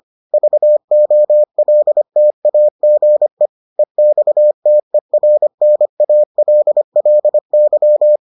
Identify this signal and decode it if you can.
Morse code, heard as a keyed tone, about 25 wpm, 600 Hz. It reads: VOLTAGE EXTERNALLY